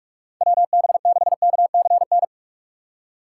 Morse code, heard as a keyed tone, about 42 wpm, 700 Hz. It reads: WB6KCN
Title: USA Call Signs - 6 Characters - 6x Character Spacing - 45wpm